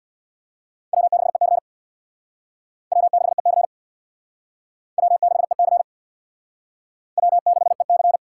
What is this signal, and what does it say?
Morse code, heard as a keyed tone, about 59 wpm, 700 Hz. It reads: W6EC W6EC W6EC W6EC
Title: Speed Racing - USA 1x2 Call Signs 50wpm